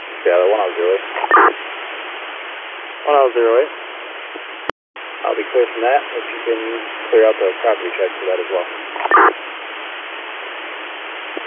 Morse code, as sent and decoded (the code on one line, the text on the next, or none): none